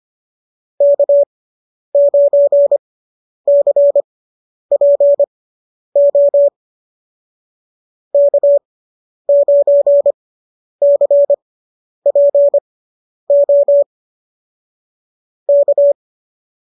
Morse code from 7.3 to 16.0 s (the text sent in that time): K9CPO K